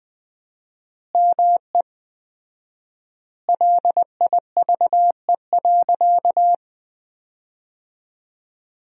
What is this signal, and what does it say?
Morse code, heard as a keyed tone, about 19 wpm, 700 Hz. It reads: ME LIVE.
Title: 4x Word-Spacing - Sentences from Top 300 Words - 20wpm